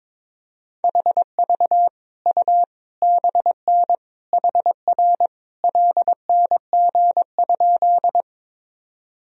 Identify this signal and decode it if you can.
Morse code, heard as a keyed tone, about 22 wpm, 700 Hz. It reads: HV U BN HR LNG?
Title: Sentences from Top 200 Words and Abbreviations 22wpm